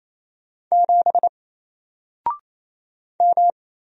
Morse code, heard as a keyed tone, about 27 wpm, 700 Hz. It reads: 7 M
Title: Single Character (Letter:Number:Punctuation:Prosign) 28wpm